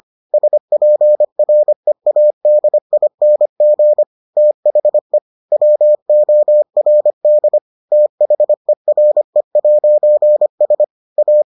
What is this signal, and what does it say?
Morse code, heard as a keyed tone, about 25 wpm, 600 Hz. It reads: SPREADING THE WORD THERE'S A